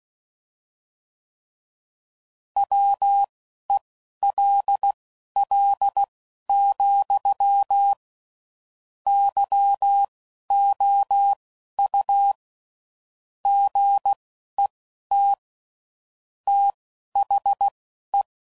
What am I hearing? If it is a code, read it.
Morse code, heard as a keyed tone, about 16 wpm, spaced as for 11 wpm, 800 Hz. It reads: WELL, YOU GET THE
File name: CW Code Groups @10WPM/Technical Subjects